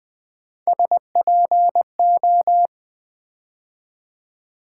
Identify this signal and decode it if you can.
Morse code, heard as a keyed tone, about 20 wpm, 700 Hz. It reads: SPO
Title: Build A Word - 3 Letters At A Time - 9-Letter Words - 20wpm